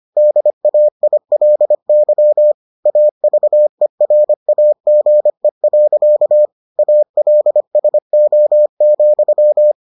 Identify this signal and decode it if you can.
Morse code, heard as a keyed tone, about 25 wpm, 600 Hz. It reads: DAILY AVERAGE. ALSO,